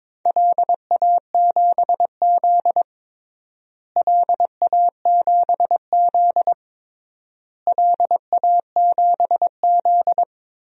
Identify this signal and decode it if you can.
Morse code, heard as a keyed tone, about 22 wpm, 700 Hz. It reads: LA7Z LA7Z LA7Z
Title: DXCC-100 Call Signs - 4 Characters - Repeated 3x - 22wpm